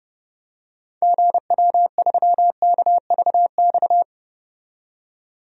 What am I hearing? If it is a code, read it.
Morse code, heard as a keyed tone, about 30 wpm, 700 Hz. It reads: GW3KVX
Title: DXCC-100 Call Signs - 6 Characters - 30wpm